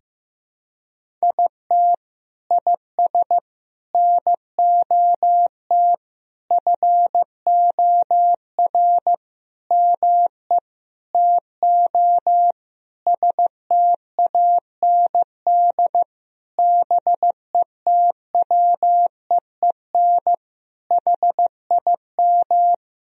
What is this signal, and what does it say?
Morse code, heard as a keyed tone, about 15 wpm, 700 Hz. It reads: IT IS NOT FOR ME TO STAND BETWEEN HIM
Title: Sentences from Top 1000 Words - Review Part 1 - 15wpm